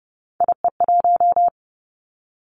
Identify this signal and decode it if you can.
Morse code, heard as a keyed tone, about 30 wpm, 700 Hz. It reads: IE1
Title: Three Challenging Letter-Number Combinations 30wpm